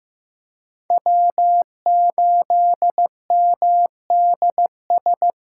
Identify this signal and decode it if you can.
Morse code, heard as a keyed tone, about 15 wpm, 700 Hz. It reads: W8MDS
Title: USA Call Signs - 2x Character Spacing - 15wpm